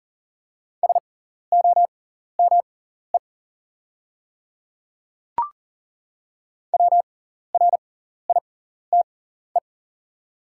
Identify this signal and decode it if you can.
Morse code, heard as a keyed tone, about 40 wpm, spaced as for 14 wpm, 690 Hz. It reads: SOME WRITE
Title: 6x Character Spacing - Top 100 Words - 40wpm